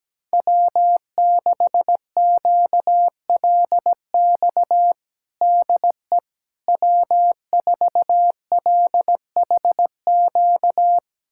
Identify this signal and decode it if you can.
Morse code, heard as a keyed tone, about 17 wpm, 700 Hz. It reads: W6QLX DE W4LHQ
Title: USA Callsign DE Callsign - 5 Characters - 17wpm